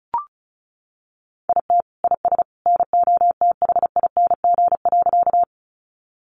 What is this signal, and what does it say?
Morse code, heard as a keyed tone, about 34 wpm, 700 Hz. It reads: IT IS NOTHING.
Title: Sentences from Top 500 Words 35wpm